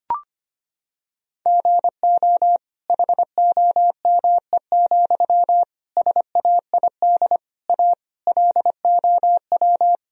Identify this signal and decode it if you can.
Morse code, heard as a keyed tone, about 25 wpm, 700 Hz. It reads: GO HOME, SAID A LOW
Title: Sentences from Top 1000 Words - Review Part 1 - 25wpm